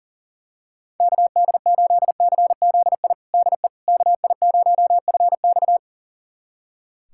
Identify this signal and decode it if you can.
Morse code, heard as a keyed tone, about 37 wpm, 700 Hz. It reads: KD8CZI DE KI0FX
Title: USA Callsign DE Callsign 40wpm